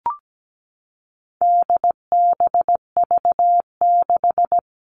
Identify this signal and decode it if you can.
Morse code, heard as a keyed tone, about 17 wpm, 700 Hz. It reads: DBV6
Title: All Cut Numbers - 4-Digit 17wpm